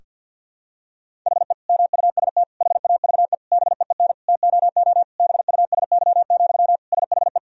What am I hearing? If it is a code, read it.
Morse code, heard as a keyed tone, about 49 wpm, 700 Hz. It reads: HE MUST HAVE BEEN TOO BUSY, SHE
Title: Sentences from Top 1000 Words - Review Part 3 - 50wpm